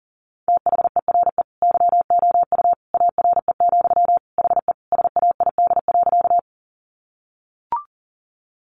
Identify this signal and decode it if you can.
Morse code, heard as a keyed tone, about 40 wpm, 700 Hz. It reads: THERE YOU ARE, HE SAID.
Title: Sentences from Top 100 Words 40wpm